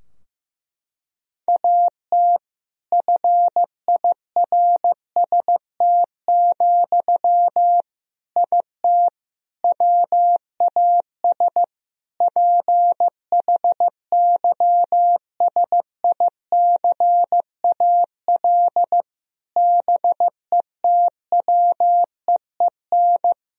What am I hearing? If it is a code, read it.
Morse code, heard as a keyed tone, about 16 wpm, 700 Hz. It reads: AT FIRST, IT WAS PHYSICAL BETWEEN
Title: Sentences from Top 1000 Words - Review Part 4 - 15wpm